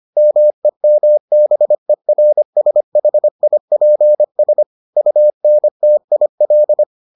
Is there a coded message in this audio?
Yes: Morse code from 0.2 to 6.8 s, heard as a keyed tone, about 25 wpm, 600 Hz. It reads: MEMBERSHIPS UNTIL